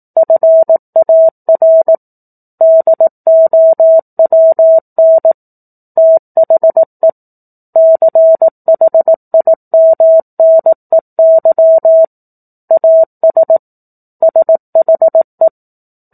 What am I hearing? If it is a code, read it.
Morse code, heard as a keyed tone, about 18 wpm, 650 Hz. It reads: FAR DOWN THE CHIMNEY AS SHE
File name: 04-Chapter-04 18 WPM